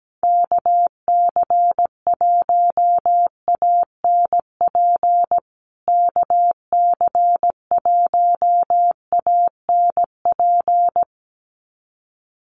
Morse code, heard as a keyed tone, about 17 wpm, 700 Hz. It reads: KC1ANP KC1ANP
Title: USA Call Signs Repeated 2x 17wpm